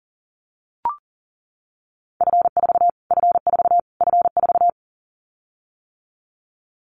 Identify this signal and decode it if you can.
Morse code, heard as a keyed tone, about 40 wpm, 700 Hz. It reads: F4 F4 F4
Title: Callsign Prefixes for Top 25 DXCC - Repeated 3x - 40wpm